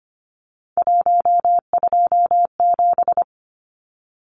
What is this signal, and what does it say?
Morse code, heard as a keyed tone, about 25 wpm, 700 Hz. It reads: 127
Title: Numbers - 3-Digit 25wpm